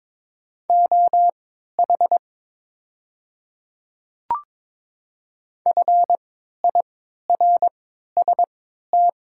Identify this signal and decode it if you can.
Morse code, heard as a keyed tone, about 23 wpm, spaced as for 13 wpm, 700 Hz. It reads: OH FIRST
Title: 3x Character Spacing - Top 500 Words - 22wpm